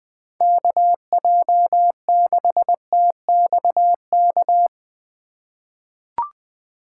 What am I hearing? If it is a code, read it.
Morse code, heard as a keyed tone, about 20 wpm, 700 Hz. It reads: KJ6TXK
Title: USA Call Signs - 6 Characters - 6x Character Spacing - 20wpm